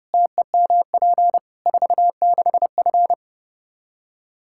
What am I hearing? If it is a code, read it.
Morse code, heard as a keyed tone, about 32 wpm, 700 Hz. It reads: TEMP 46F
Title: Practice QSO - Exchange 2 - Info - 30wpm